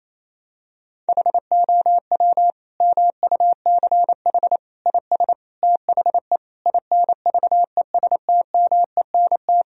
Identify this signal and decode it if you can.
Morse code, heard as a keyed tone, about 28 wpm, 700 Hz. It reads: HOW MUCH IS THE INVESTMENT
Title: Sentences from Top 1000 Words - Review Part 4 - 28wpm